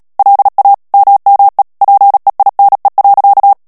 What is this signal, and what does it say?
Morse code, heard as a keyed tone, about 37 wpm, 800 Hz. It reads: LA MME PEINE.
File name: CW / FICHIERS-MP3 / FRANCAIS TEXTE CLAIR 2 VITESSE 35 MOTS MINUTES